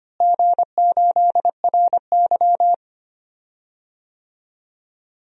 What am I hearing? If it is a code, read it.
Morse code, heard as a keyed tone, about 25 wpm, 700 Hz. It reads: G8RY
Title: DXCC-100 Call Signs - 4 Characters - Repeated 2x - 25wpm